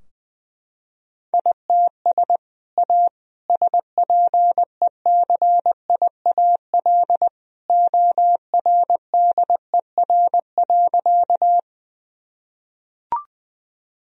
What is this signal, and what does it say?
Morse code, heard as a keyed tone, about 20 wpm, 700 Hz. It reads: ITS A SPECIAL ORDER.
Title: Sentences from Top 1000 Words - Review Part 3 - 20wpm